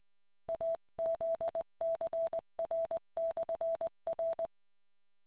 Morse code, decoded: AZCR/R